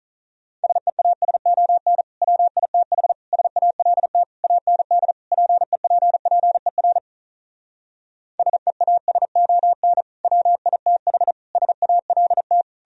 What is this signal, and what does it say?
Morse code, heard as a keyed tone, about 46 wpm, 700 Hz. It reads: SEASON WITH SALT AND PEPPER SEASON WITH SALT